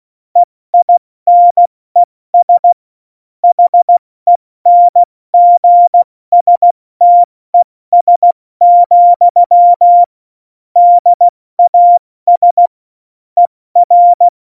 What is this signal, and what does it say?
Morse code, heard as a keyed tone, about 16 wpm, 700 Hz. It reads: EINES HENGSTES, DAS ER
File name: Morsekurs / Witze / Witz 28 Juli 2020, 70 F80